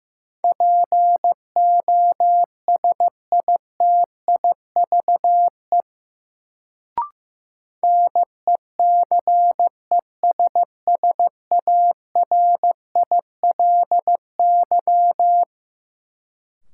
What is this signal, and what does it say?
Morse code, heard as a keyed tone, about 15 wpm, 700 Hz. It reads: POSITIVE NECESSARILY
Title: Sentences from Top 1400 Words - New Word Drill Alt - 15wpm